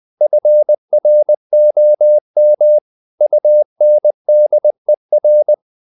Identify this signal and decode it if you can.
Morse code, heard as a keyed tone, about 20 wpm, 600 Hz. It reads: FROM UNDER
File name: the old man and the sea part 2 20wpm